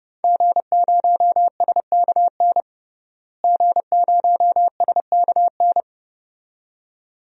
Morse code, heard as a keyed tone, about 30 wpm, 700 Hz. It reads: G0SKN G0SKN
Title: DXCC-100 Call Signs - 5 Characters - Repeated 2x - 30wpm